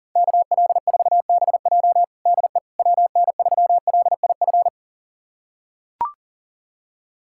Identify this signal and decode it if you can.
Morse code, heard as a keyed tone, about 40 wpm, 700 Hz. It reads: KL4BJ DE WN3LIF
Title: USA Callsign DE Callsign 40wpm